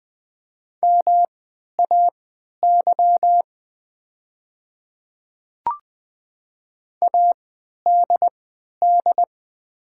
Morse code, heard as a keyed tone, about 20 wpm, spaced as for 11 wpm, 700 Hz. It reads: MAY ADD
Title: 3x Character Spacing - Top 300 Words - 20wpm